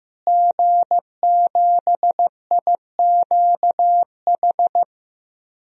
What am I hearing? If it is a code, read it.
Morse code, heard as a keyed tone, about 15 wpm, 700 Hz. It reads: G7IQH
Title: DXCC-100 Call Signs - 5 Characters - Repeated 2x - 15wpm